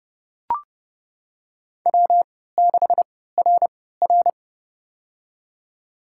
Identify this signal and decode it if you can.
Morse code, heard as a keyed tone, about 30 wpm, spaced as for 17 wpm, 700 Hz. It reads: W6RR
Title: USA Call Signs - 4 Characters - 3x Character Spacing - 30wpm